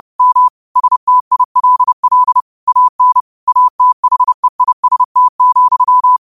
Morse code, heard as a keyed tone, about 30 wpm, 1000 Hz. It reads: M STILL AN ATHEIST,